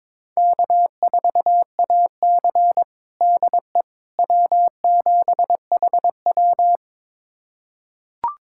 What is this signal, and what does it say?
Morse code, heard as a keyed tone, about 22 wpm, 700 Hz. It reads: K4AC DE W7HW
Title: USA Callsign DE Callsign - 4 Characters - 22wpm